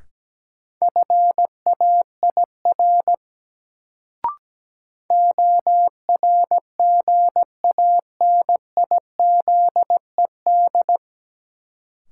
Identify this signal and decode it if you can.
Morse code, heard as a keyed tone, about 17 wpm, 700 Hz. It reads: FAIR ORGANIZED
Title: Sentences from Top 1300 Words - New Word Drill Alt - 17wpm